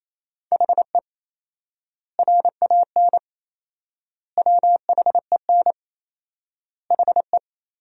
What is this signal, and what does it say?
Morse code, heard as a keyed tone, about 28 wpm, 700 Hz. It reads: HE RAN WHEN HE
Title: 4x Word-Spacing - Sentences from Top 500 Words - 28wpm